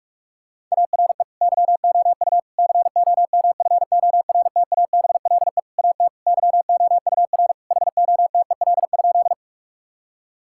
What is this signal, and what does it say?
Morse code, heard as a keyed tone, about 41 wpm, 700 Hz. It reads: ARE YOU COMFORTABLE AT YOUR HOTEL?
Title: Sentences from Top 1600 Words 45wpm